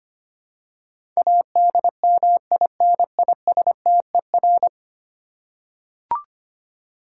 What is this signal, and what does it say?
Morse code, heard as a keyed tone, about 25 wpm, 700 Hz. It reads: ADMINISTER
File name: 10 Letter Words - First 3 Letters Given - 25wpm